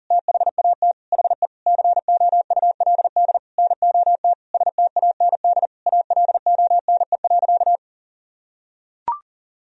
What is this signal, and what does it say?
Morse code, heard as a keyed tone, about 40 wpm, 700 Hz. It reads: THAT HE COULD NOT STAND ALONE.
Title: Sentences from Top 1000 Words - Review Part 3 - 40wpm